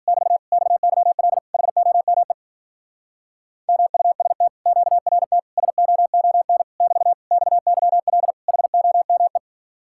Morse code, heard as a keyed tone, about 48 wpm, 700 Hz. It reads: <BT> XYL HOME MUST QRT SOON <BT> XYL HOME
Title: Speed-Racing Common QSO Phrases - 40wpm